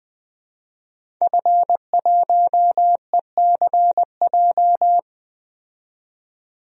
Morse code, heard as a keyed tone, about 20 wpm, 700 Hz. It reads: F1ECJ
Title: DXCC-100 Call Signs - 5 Characters - 20wpm